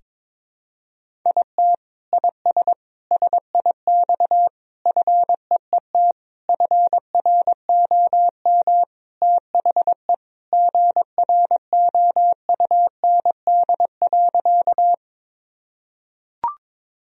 Morse code, heard as a keyed tone, about 22 wpm, 700 Hz. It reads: IT IS SIX FEET FROM THE GROUND.
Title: Sentences from Top 1000 Words - Review Part 2 - 22wpm